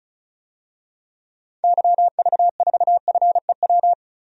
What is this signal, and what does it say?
Morse code, heard as a keyed tone, about 36 wpm, 700 Hz. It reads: YV4FEW
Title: DXCC-100 Call Signs - 6 Characters - 35wpm